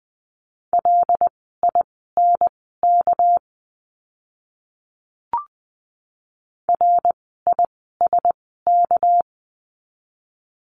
Morse code, heard as a keyed tone, about 20 wpm, spaced as for 14 wpm, 700 Hz. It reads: LINK RISK
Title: 199 Lesson 32f - Letter k - Words 20wpm